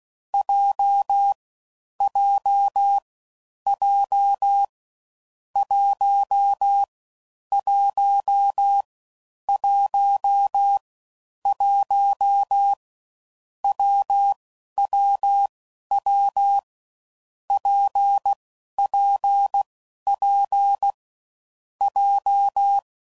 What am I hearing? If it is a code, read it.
Morse code, heard as a keyed tone, about 16 wpm, 800 Hz. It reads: J J J 1 1 1 1 W W W P P P J
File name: Combinations / Difficult Combos / WPJ WPJ1